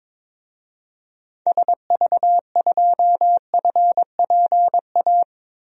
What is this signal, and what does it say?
Morse code, heard as a keyed tone, about 22 wpm, 700 Hz. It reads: SV2FPA
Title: DXCC-100 Call Signs - 6 Characters - Repeated 3x - 22wpm